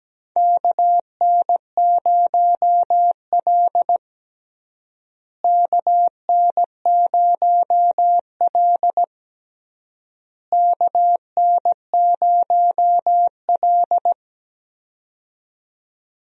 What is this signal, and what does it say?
Morse code, heard as a keyed tone, about 17 wpm, 700 Hz. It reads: KN0L KN0L KN0L
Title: USA Call Signs - 4 Characters - Repeated 3x - 17wpm